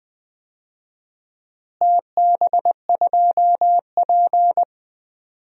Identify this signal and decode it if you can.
Morse code, heard as a keyed tone, about 20 wpm, 700 Hz. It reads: TB2P
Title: DXCC-100 Call Signs - 4 Characters - Repeated 2x - 20wpm